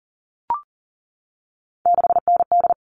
Morse code, heard as a keyed tone, about 40 wpm, 700 Hz. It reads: 6ND
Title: All Cut Numbers - 3-Digit 40wpm